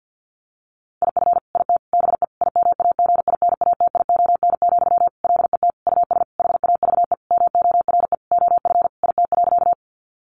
Morse code, heard as a keyed tone, about 50 wpm, 700 Hz. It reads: IF IT BE IMAGINATION, LET US HAVE MORE OF IT.
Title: Sentences from Top 1300 Words 50wpm